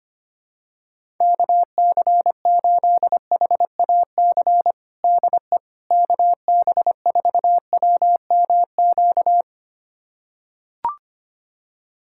Morse code, heard as a keyed tone, about 25 wpm, 700 Hz. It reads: KC8HAC DE KB4WMQ
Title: USA Callsign DE Callsign - 6 Characters - 25wpm